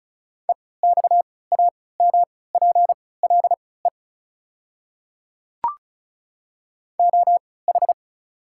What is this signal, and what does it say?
Morse code, heard as a keyed tone, about 36 wpm, spaced as for 20 wpm, 690 Hz. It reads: EXAMPLE OH
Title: 3x Character Spacing - Top 500 Words - 35wpm